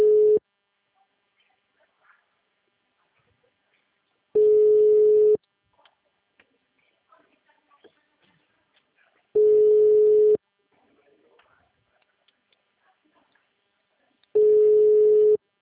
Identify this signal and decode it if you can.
Ringback tone (Europe, 425 Hz), with speech.
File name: out- -2014